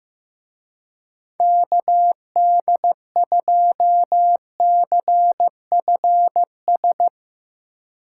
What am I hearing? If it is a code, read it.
Morse code, heard as a keyed tone, about 16 wpm, 700 Hz. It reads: KD2CFS
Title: Speed Racing - USA 2x3 Call Signs 15wpm